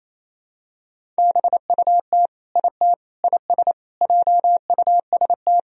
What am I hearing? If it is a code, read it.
Morse code, heard as a keyed tone, about 28 wpm, 700 Hz. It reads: BUT IT IS JUST
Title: Sentences from Top 200 Words and Abbreviations 28wpm